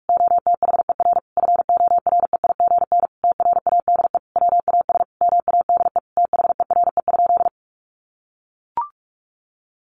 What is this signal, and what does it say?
Morse code, heard as a keyed tone, about 44 wpm, 700 Hz. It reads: OTHER FOREIGN TRADE WAS MADE THERE?
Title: Sentences from Top 1400 Words 45wpm